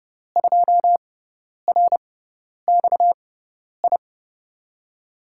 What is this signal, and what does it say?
Morse code, heard as a keyed tone, about 30 wpm, spaced as for 10 wpm, 700 Hz. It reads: 2RXI
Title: USA Call Signs - 6 Characters - 6x Character Spacing - 30wpm